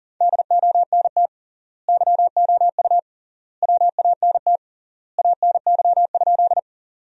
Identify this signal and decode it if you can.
Morse code, heard as a keyed tone, about 40 wpm, 700 Hz. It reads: DONT YOU WANT ANY?
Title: 3x Word-Spacing - Sentences from Top 300 Words - 40wpm